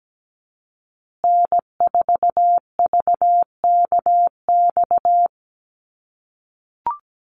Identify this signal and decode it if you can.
Morse code, heard as a keyed tone, about 17 wpm, 700 Hz. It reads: N4VKX
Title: USA Call Signs Repeated 3x 17wpm